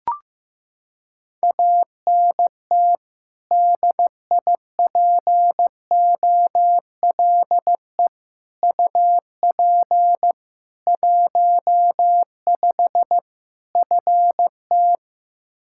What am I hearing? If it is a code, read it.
Morse code, heard as a keyed tone, about 15 wpm, 700 Hz. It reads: ANT DIPOLE UP 15 FT
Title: K1USN SST QSO Practice 15wpm